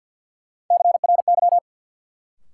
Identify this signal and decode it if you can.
Morse code, heard as a keyed tone, about 52 wpm, 700 Hz. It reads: CRY
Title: Sentences from Top 500 Words - New Word Drill - 50wpm